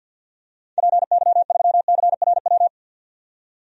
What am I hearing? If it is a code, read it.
Morse code, heard as a keyed tone, about 46 wpm, 700 Hz. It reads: PY3CRW
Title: DXCC-100 Call Signs - 6 Characters - 50wpm